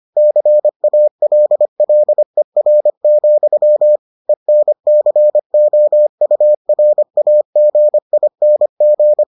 Morse code, heard as a keyed tone, about 25 wpm, 600 Hz. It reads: CALLER, ENCOURAGING